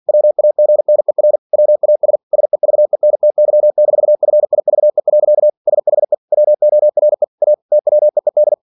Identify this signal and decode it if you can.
Morse code, heard as a keyed tone, about 49 wpm, 600 Hz. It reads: WAGNER WAS SEVENTY-FIVE. SHE WORE A TWEED
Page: Never 47wpm